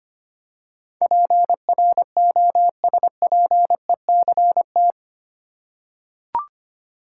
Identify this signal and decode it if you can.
Morse code, heard as a keyed tone, about 25 wpm, 700 Hz. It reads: PROSPECT
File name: Eight Letter Words - 25wpm